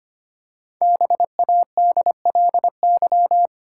Morse code, heard as a keyed tone, about 25 wpm, 700 Hz. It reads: BADLY
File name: 5 Letter Words - First 3 Letters Given - 25wpm